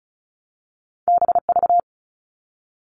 Morse code, heard as a keyed tone, about 36 wpm, 700 Hz. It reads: BV
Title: All Cut Numbers - 2-Digit 35wpm